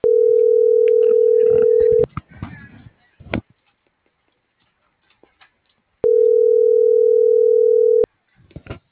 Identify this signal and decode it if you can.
Ringback tone (North America), 440 + 480 Hz.